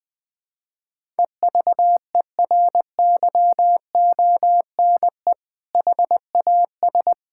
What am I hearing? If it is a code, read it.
Morse code, heard as a keyed tone, about 21 wpm, 700 Hz. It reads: EVERYONE HAS
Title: Sentences from Top 1000 Words - Review Part 5 - 20wpm